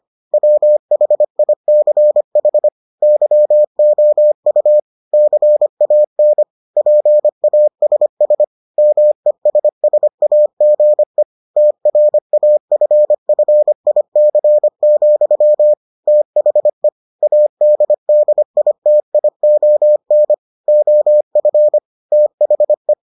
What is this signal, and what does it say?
Morse code, heard as a keyed tone, about 25 wpm, 600 Hz. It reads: WHICH YOU CAN PASS MESSAGE TRAFFIC, THE ADDITION OF THE